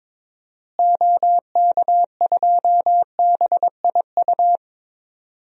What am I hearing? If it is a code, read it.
Morse code, heard as a keyed tone, about 23 wpm, 700 Hz. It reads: OK2BIU
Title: DXCC-100 Call Signs - 6 Characters - Repeated 2x - 22wpm